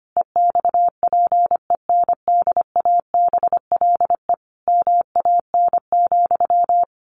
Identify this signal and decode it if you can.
Morse code, heard as a keyed tone, about 25 wpm, 700 Hz. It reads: EXPENDABLE MAN,
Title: A Sound Of Thunder 25wpm